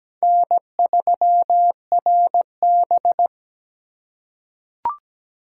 Morse code, heard as a keyed tone, about 17 wpm, 700 Hz. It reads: N3RB
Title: USA Call Signs - 2x Character Spacing - 17wpm